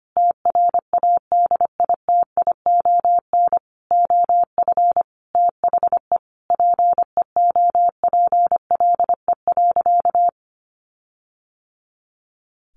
Morse code, heard as a keyed tone, about 25 wpm, 700 Hz. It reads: TRADITION OF THE PEOPLE.
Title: Sentences from Top 1500 Words 25wpm